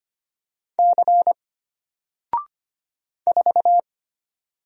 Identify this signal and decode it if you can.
Morse code, heard as a keyed tone, about 25 wpm, 700 Hz. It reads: C 4
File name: Single-Letter-Number-25wpm